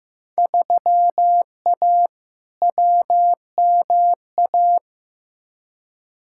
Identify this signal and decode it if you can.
Morse code, heard as a keyed tone, about 15 wpm, 700 Hz. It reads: 3A WMA
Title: Field Day Exchanges Encore - 15wpm